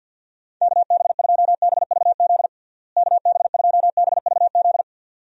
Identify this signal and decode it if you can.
Morse code, heard as a keyed tone, about 51 wpm, 700 Hz. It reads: KB2BVZ KB2BVZ
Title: USA Call Signs - 6 Characters - Repeated 2x - 50wpm